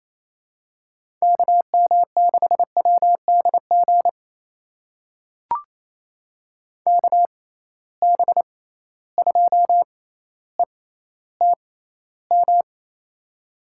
Morse code, heard as a keyed tone, about 28 wpm, spaced as for 14 wpm, 700 Hz. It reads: KM6WDG KB2ETM
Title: USA Call Signs - 6 Characters - 6x Character Spacing - 28wpm